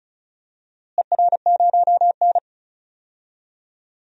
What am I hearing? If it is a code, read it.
Morse code, heard as a keyed tone, about 36 wpm, 700 Hz. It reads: ER0N
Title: DXCC-100 Call Signs - 4 Characters - Repeated 2x - 35wpm